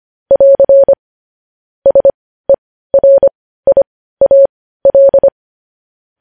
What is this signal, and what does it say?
Morse code, heard as a keyed tone, about 25 wpm, spaced as for 15 wpm, 550 Hz. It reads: <AR> SERIAL
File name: podcast / News headlines-1x-25x15